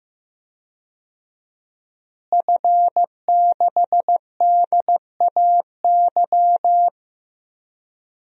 Morse code, heard as a keyed tone, about 15 wpm, 700 Hz. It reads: F6DAY
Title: DXCC-100 Call Signs - 5 Characters - Repeated 3x - 15wpm